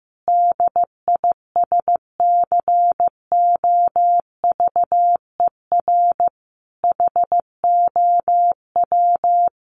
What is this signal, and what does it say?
Morse code, heard as a keyed tone, about 15 wpm, 700 Hz. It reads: DISCOVER HOW
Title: Sentences from Top 1100 Words 15wpm